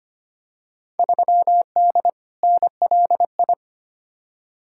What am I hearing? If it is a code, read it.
Morse code, heard as a keyed tone, about 25 wpm, 700 Hz. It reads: 3D NLI
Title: Field Day Exchanges Encore - 25wpm